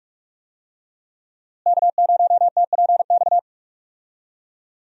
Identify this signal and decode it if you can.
Morse code, heard as a keyed tone, about 43 wpm, 700 Hz. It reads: K0TPX
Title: USA Call Signs - 5 Characters - Repeated 2x - 45wpm